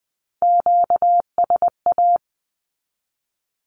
Morse code, heard as a keyed tone, about 20 wpm, 700 Hz. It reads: QSA
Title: Q-Codes 20wpm